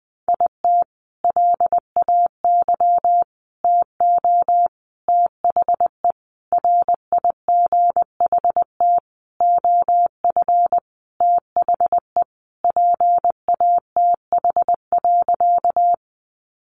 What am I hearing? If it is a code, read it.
Morse code, heard as a keyed tone, about 19 wpm, 700 Hz. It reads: IT LAY TO THE RIGHT OF THE PATH.
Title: Sentences from Top 1400 Words 20wpm